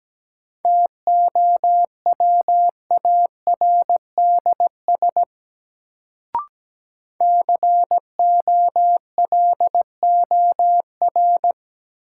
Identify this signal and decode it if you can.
Morse code, heard as a keyed tone, about 17 wpm, 700 Hz. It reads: TOWARDS COLOR
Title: Sentences from Top 1000 Words - Review Part 2 - New Word Drill - 17wpm